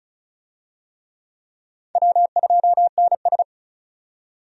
Morse code, heard as a keyed tone, about 34 wpm, 700 Hz. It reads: W2NS
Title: USA Call Signs - 4 Characters - 35wpm